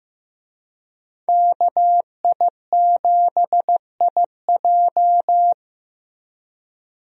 Morse code, heard as a keyed tone, about 16 wpm, 700 Hz. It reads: KI7IJ
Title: USA Call Signs - 5 Characters - Repeated 3x - 15wpm